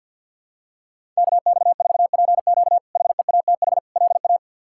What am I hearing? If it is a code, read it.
Morse code, heard as a keyed tone, about 52 wpm, 700 Hz. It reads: KX4PQ HEATH LA
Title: K1USN SST QSO Practice - 50wpm